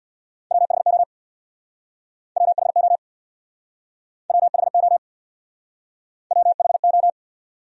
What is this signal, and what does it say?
Morse code, heard as a keyed tone, about 67 wpm, 700 Hz. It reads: WHO WHO WHO WHO